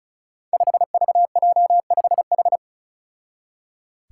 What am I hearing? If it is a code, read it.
Morse code, heard as a keyed tone, about 36 wpm, 700 Hz. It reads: 5VJ5H
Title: Five Challenging Letter-Number Combinations 35wpm